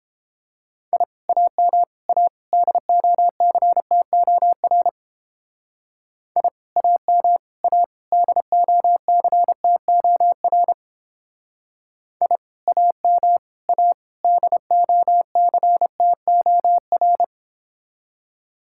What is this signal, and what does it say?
Morse code, heard as a keyed tone, about 29 wpm, 700 Hz. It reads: I AM A DOCTOR I AM A DOCTOR I AM A DOCTOR